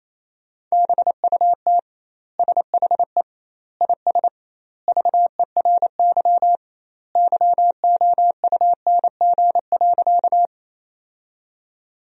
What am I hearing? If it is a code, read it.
Morse code, heard as a keyed tone, about 28 wpm, 700 Hz. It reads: BUT SHE IS VERY YOUNG.
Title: 2x Word-Spacing - Sentences from Top 400 Words - 28wpm